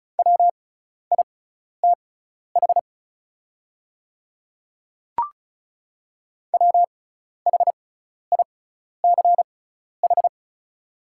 Morse code, heard as a keyed tone, about 36 wpm, spaced as for 12 wpm, 700 Hz. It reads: WITH WHICH
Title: 6x Character Spacing - Top 100 Words - 35wpm